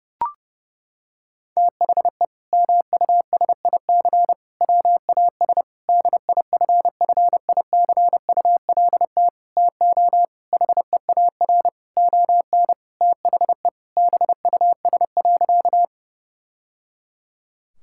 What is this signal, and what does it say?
Morse code, heard as a keyed tone, about 32 wpm, 700 Hz. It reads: THE MUSIC WAS DIFFICULT TO HEAR ON THE BUS.
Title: Sentences from Top 1000 Words - Review Part 6 - 30wpm